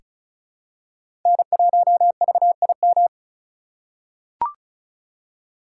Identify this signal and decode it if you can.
Morse code, heard as a keyed tone, about 36 wpm, 700 Hz. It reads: N1VIM
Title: USA Call Signs - 5 Characters - 4x Character Spacing - 35wpm